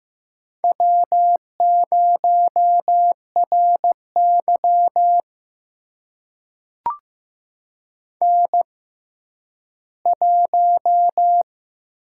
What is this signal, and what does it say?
Morse code, heard as a keyed tone, about 15 wpm, spaced as for 10 wpm, 700 Hz. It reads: W0RY N1
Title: USA Call Signs - 4 Characters - 6x Character Spacing - 15wpm